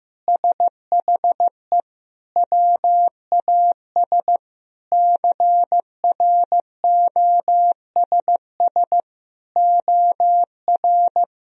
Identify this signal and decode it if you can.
Morse code, heard as a keyed tone, about 15 wpm, 700 Hz. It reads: SHE WAS CROSS OR
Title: Sentences from Top 1000 Words - Review Part 3 - 15wpm